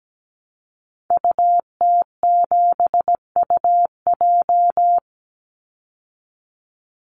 Morse code, heard as a keyed tone, about 17 wpm, 700 Hz. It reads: UT7UJ
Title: Call Signs Repeated 2x - 17wpm